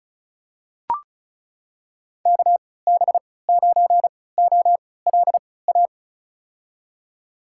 Morse code, heard as a keyed tone, about 36 wpm, spaced as for 20 wpm, 700 Hz. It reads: KB9OLA
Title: USA Call Signs - 6 Characters - 3x Character Spacing - 35wpm